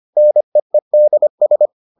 Morse code, heard as a keyed tone, about 25 wpm, 600 Hz. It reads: NEEDS